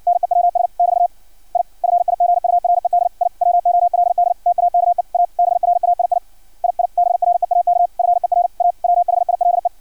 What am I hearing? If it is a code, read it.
Morse code, heard as a keyed tone, about 56 wpm, 700 Hz. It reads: NEWS <BT> I FIGURED I WOLD TAKE A BRAII II BREAK FER A WHILE
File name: archived / CRSnet